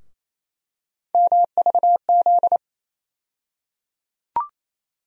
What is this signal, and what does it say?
Morse code, heard as a keyed tone, about 27 wpm, 700 Hz. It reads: MVZ
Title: USA Call Sign Postfixes - 3 Letters 28wpm